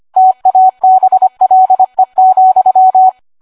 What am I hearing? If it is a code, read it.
Morse code, heard as a keyed tone, about 25 wpm, 750 Hz. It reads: TABLE,